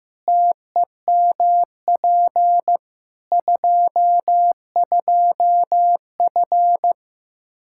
Morse code, heard as a keyed tone, about 15 wpm, 700 Hz. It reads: TEMP 22F
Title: Practice QSO - Exchange 2 - Info - 15wpm